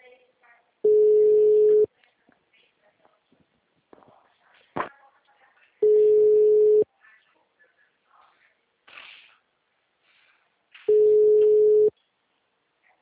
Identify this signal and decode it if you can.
Ringback tone (Europe, 425 Hz), with speech.
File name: out- -2054